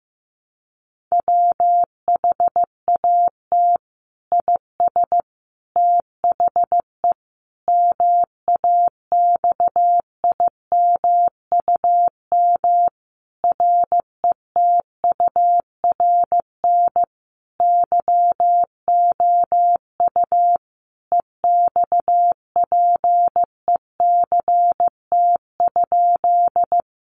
Morse code, heard as a keyed tone, about 15 wpm, 700 Hz. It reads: WHAT IS THE MAXIMUM RETURN YOU EXPECT?
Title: Sentences from Top 1300 Words 15wpm